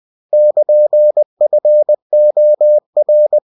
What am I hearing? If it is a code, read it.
Morse code, heard as a keyed tone, about 20 wpm, 600 Hz. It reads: <KN>FOR